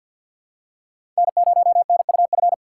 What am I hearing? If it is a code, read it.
Morse code, heard as a keyed tone, about 52 wpm, 700 Hz. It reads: N0NUF
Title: USA Call Signs - 5 Characters - 2x Character Spacing - 50wpm